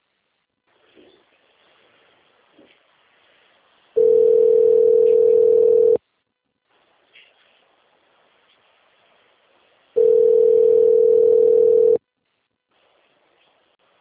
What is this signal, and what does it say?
Ringback tone (North America), 440 + 480 Hz.